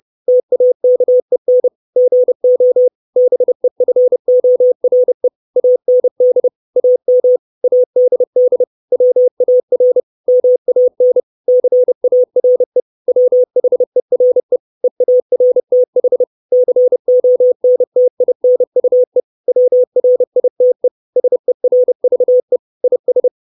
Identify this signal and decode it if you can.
Morse code, heard as a keyed tone, about 30 wpm, 500 Hz. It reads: TAKEN GO BEFORE AND AM ADD WAR MAN CARE WHERE EARTH CONTINUE WRITE SERVE IS
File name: 632 Words 30WPM